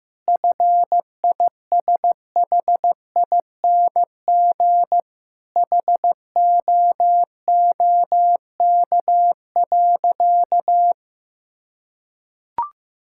Morse code, heard as a keyed tone, about 15 wpm, 700 Hz. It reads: FISHING HOOK.
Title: Sentences from Top 1600 Words 15wpm